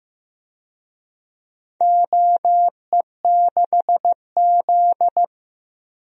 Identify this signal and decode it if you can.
Morse code, heard as a keyed tone, about 15 wpm, 700 Hz. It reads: OE6Z
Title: DXCC-100 Call Signs - 4 Characters - Repeated 3x - 15wpm